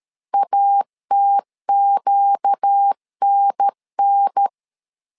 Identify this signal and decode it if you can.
Morse code, heard as a keyed tone, about 13 wpm, 800 Hz. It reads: ATQNN
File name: LEZIONI / 4 Q 12wpm 1833